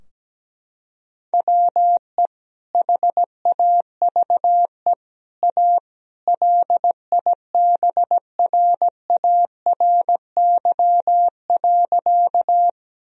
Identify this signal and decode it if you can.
Morse code, heard as a keyed tone, about 17 wpm, 700 Hz. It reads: WE HAVE A LIBRARY.
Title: Sentences from Top 1000 Words - Review Part 4 - 17wpm